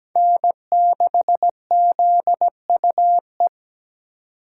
Morse code, heard as a keyed tone, about 20 wpm, 700 Hz. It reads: N6ZUE
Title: Speed Racing - USA Call Signs 17wpm